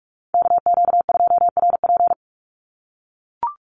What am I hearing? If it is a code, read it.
Morse code, heard as a keyed tone, about 44 wpm, 700 Hz. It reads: KQ2RP
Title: NA SOTA Chaser Callsigns 45wpm